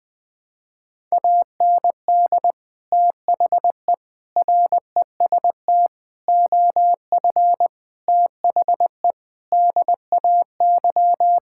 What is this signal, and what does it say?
Morse code, heard as a keyed tone, about 20 wpm, 700 Hz. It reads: AND THE REST OF THE DAY
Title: Sentences with 4-Word Phrases - 20wpm